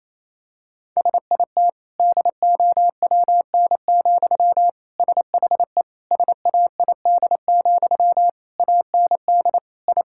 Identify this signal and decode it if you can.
Morse code, heard as a keyed tone, about 28 wpm, 700 Hz. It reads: SIT DOWN, SHE SAID, AND I
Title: Sentences from Top 1000 Words - Review Part 6 - 28wpm